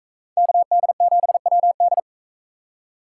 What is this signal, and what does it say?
Morse code, heard as a keyed tone, about 42 wpm, 700 Hz. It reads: KD7WD